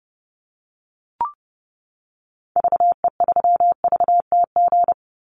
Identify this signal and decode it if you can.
Morse code, heard as a keyed tone, about 30 wpm, 700 Hz. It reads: VE3VTG
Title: Call Signs - 30wpm